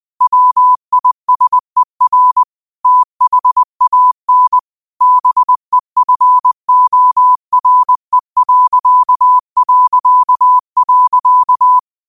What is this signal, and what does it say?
Morse code, heard as a keyed tone, about 20 wpm, 1000 Hz. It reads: WISER THAN BEFORE...
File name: qotdfiles / 20wpm